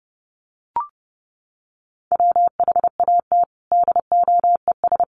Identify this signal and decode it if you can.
Morse code, heard as a keyed tone, about 30 wpm, 700 Hz. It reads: WHAT DOES
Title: Sentences from Top 1400 Words 30wpm